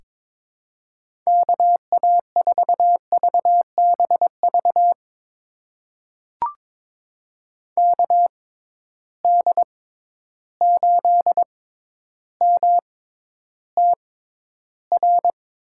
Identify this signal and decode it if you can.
Morse code, heard as a keyed tone, about 22 wpm, spaced as for 11 wpm, 700 Hz. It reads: KA4VBV KD8MTR
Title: USA Call Signs - 6 Characters - 6x Character Spacing - 22wpm